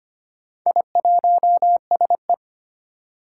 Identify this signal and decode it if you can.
Morse code, heard as a keyed tone, about 25 wpm, 700 Hz. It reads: I1SE
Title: Four Challenging Letter-Number Combinations 25wpm